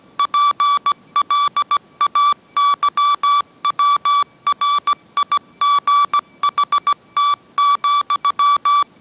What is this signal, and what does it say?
Morse code, heard as a keyed tone, about 17 wpm, 1150 Hz. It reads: PLAYWRIGHT,